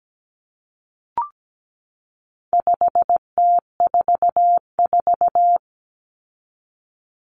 Morse code, heard as a keyed tone, about 17 wpm, 700 Hz. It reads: 5T44
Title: Common Cut Numbers - 4-Digit 17wpm